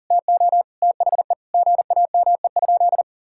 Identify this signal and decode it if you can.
Morse code, heard as a keyed tone, about 40 wpm, 700 Hz. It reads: TO THE GAME?
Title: Sentences from Top 1000 Words - Review Part 6 - 40wpm